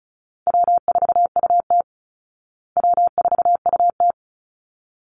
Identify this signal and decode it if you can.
Morse code, heard as a keyed tone, about 34 wpm, 700 Hz. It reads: W4UT W4UT
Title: Easy Call Signs Repeated 2x - 35wpm